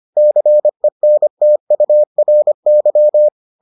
Morse code, heard as a keyed tone, about 25 wpm, 600 Hz. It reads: CENTURY